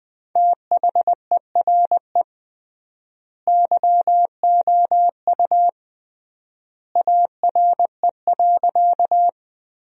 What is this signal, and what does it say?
Morse code, heard as a keyed tone, about 20 wpm, 700 Hz. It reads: THERE YOU ARE.
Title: 3x Word-Spacing - Sentences from Top 100 Words - 20wpm